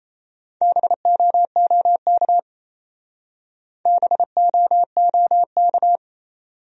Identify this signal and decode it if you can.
Morse code, heard as a keyed tone, about 34 wpm, 700 Hz. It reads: BOOK BOOK
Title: Speed-Racing Top 300 Words 28wpm